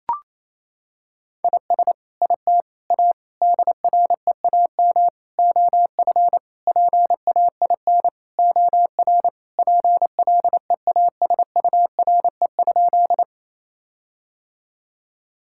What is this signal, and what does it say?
Morse code, heard as a keyed tone, about 28 wpm, 700 Hz. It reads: IS IT A DREAM OF PAIN OR PLEASURE?
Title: Sentences from Top 1600 Words 28wpm